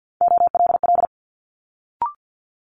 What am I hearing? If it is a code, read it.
Morse code, heard as a keyed tone, about 50 wpm, 700 Hz. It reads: OLL
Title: Multi-letter Phonemes Quick Fire 50wpm